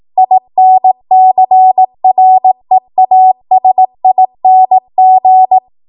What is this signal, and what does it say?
Morse code, heard as a keyed tone, about 18 wpm, 750 Hz. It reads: INCREASING
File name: Archive / 18 WPM / 18WPM